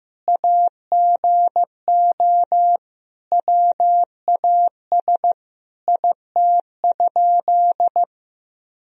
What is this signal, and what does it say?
Morse code, heard as a keyed tone, about 15 wpm, 700 Hz. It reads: AGO WAS IT?
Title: Sentences from Top 1000 Words - Review Part 3 - 15wpm